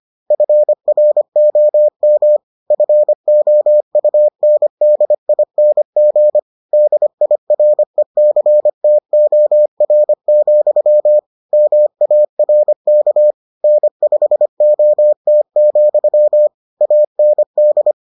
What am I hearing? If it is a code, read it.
Morse code, heard as a keyed tone, about 25 wpm, 600 Hz. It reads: FROM FOUNDING DIRECTOR, MARK N5OT, AND